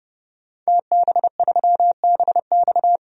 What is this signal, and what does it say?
Morse code, heard as a keyed tone, about 31 wpm, 700 Hz. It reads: TB3BX
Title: DXCC-100 Call Signs - 5 Characters - 30wpm